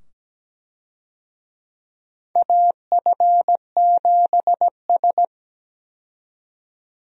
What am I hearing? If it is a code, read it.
Morse code, heard as a keyed tone, about 17 wpm, 700 Hz. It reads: AF7S
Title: USA Call Signs - 4 Characters - 17wpm